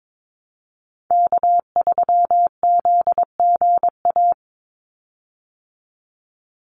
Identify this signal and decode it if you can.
Morse code, heard as a keyed tone, about 22 wpm, 700 Hz. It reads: K3ZGA
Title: Call Signs Repeated 3x - 22wpm